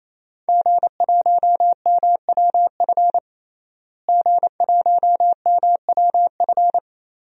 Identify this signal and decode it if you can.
Morse code, heard as a keyed tone, about 28 wpm, 700 Hz. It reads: G1MWF G1MWF
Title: DXCC-100 Call Signs - 5 Characters - Repeated 3x - 28wpm